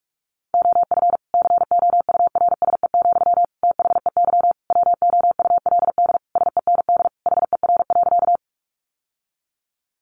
Morse code, heard as a keyed tone, about 44 wpm, 700 Hz. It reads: OF COURSE, THEY WOULD SEND HER.
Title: Sentences from Top 1500 Words 45wpm